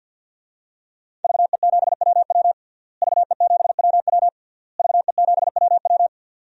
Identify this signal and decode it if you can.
Morse code, heard as a keyed tone, about 52 wpm, 700 Hz. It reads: VE7WW VE7WW VE7WW
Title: DXCC-100 Call Signs - 5 Characters - Repeated 3x - 50wpm